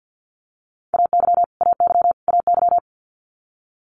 Morse code, heard as a keyed tone, about 50 wpm, 690 Hz. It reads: AY AY AY
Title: Multi-letter Phonemes Repeated 3x 50wpm